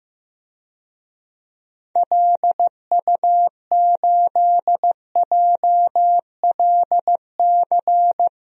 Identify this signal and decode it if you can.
Morse code, heard as a keyed tone, about 15 wpm, 700 Hz. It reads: LU8JLC
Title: DXCC-100 Call Signs - 6 Characters - Repeated 2x - 15wpm